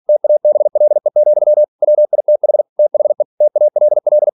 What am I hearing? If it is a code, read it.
Morse code, heard as a keyed tone, about 46 wpm, 600 Hz. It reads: TABLE, WITH THE TALL
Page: Never 47wpm